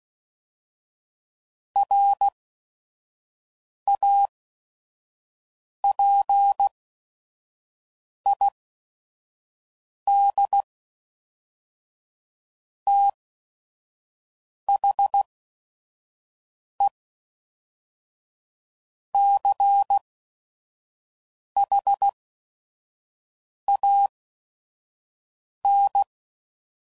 Morse code, heard as a keyed tone, about 16 wpm, 800 Hz. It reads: R A P I D T H E C H A N